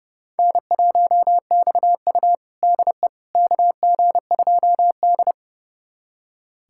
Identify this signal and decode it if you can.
Morse code, heard as a keyed tone, about 32 wpm, 700 Hz. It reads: N1XU DE KG2D
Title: USA Callsign DE Callsign - 4 Characters - 30wpm